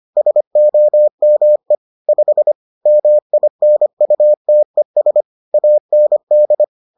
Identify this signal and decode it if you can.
Morse code, heard as a keyed tone, about 25 wpm, 600 Hz. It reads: SOME 5 MINUTES AND